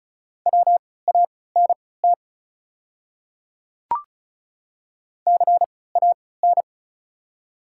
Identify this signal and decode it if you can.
Morse code, heard as a keyed tone, about 36 wpm, spaced as for 20 wpm, 700 Hz. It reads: WANT CAN
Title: 3x Character Spacing - Top 200 Words - 35wpm